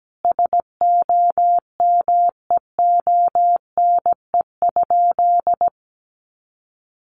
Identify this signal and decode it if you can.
Morse code, heard as a keyed tone, about 17 wpm, 700 Hz. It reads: SOMEONE?
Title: Sentences from Top 600 Words 17wpm